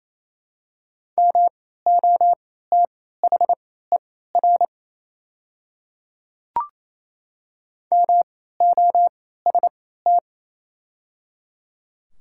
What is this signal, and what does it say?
Morse code, heard as a keyed tone, about 28 wpm, spaced as for 16 wpm, 700 Hz. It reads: MOTHER MOST
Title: 3x Character Spacing - Top 300 Words - 28wpm